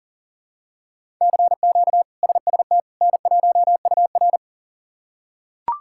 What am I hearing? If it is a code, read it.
Morse code, heard as a keyed tone, about 40 wpm, 700 Hz. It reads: CQ SST N1UR
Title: K1USN SST QSO Practice - 40wpm